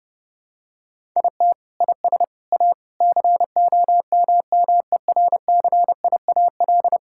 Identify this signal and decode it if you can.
Morse code, heard as a keyed tone, about 30 wpm, 700 Hz. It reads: IT IS A COMMERCIAL
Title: Sentences from Top 1000 Words - Review Part 6 - 30wpm